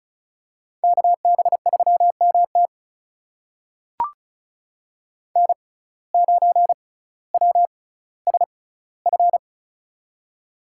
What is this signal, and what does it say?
Morse code, heard as a keyed tone, about 36 wpm, spaced as for 18 wpm, 700 Hz. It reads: KB3MT N9WSF
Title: USA Call Signs - 5 Characters - 6x Character Spacing - 35wpm